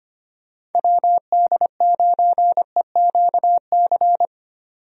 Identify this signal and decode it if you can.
Morse code, heard as a keyed tone, about 25 wpm, 700 Hz. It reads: WD9EQC
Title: USA Call Signs - 6 Characters - 25wpm